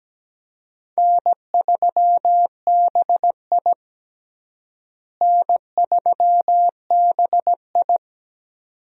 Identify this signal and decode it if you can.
Morse code, heard as a keyed tone, about 17 wpm, 700 Hz. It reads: N3BI N3BI
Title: USA Call Signs - 4 Characters - Repeated 2x - 17wpm